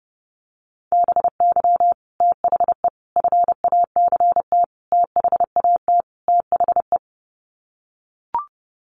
Morse code, heard as a keyed tone, about 30 wpm, 700 Hz. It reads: BY THE FACT THAT THE
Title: Sets of 5 Words Encore 30wpm